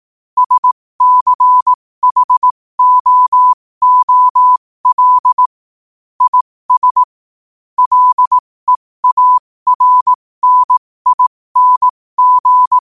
Morse code, heard as a keyed tone, about 18 wpm, spaced as for 15 wpm, 1000 Hz. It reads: SCHOOL IS LEARNING
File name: qotdfiles / 15wpm